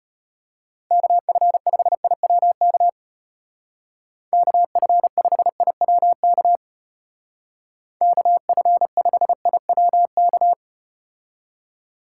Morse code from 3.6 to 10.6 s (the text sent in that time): KF5IWK KF5IWK